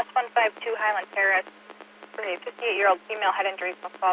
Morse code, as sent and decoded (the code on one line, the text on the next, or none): none